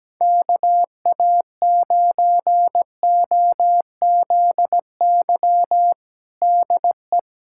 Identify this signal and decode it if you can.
Morse code, heard as a keyed tone, about 17 wpm, 700 Hz. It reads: KA9OZY DE
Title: USA Callsign DE Callsign - 6 Characters - 17wpm